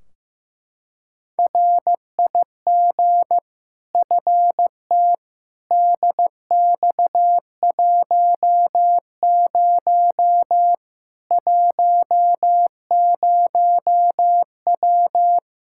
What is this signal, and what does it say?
Morse code, heard as a keyed tone, about 16 wpm, 700 Hz. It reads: RIG FT DX10 10W
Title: Practice QSO - Exchange 2 - Info - 15wpm